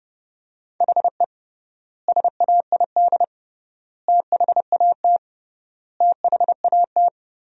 Morse code, heard as a keyed tone, about 30 wpm, 700 Hz. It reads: HE SAID THAT THAT
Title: 3x Word-Spacing - Sentences from Top 200 Words - 30wpm